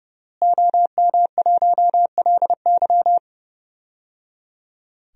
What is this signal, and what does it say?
Morse code, heard as a keyed tone, about 30 wpm, 700 Hz. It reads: OM1LY
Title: DXCC-100 Call Signs - 5 Characters - Repeated 3x - 30wpm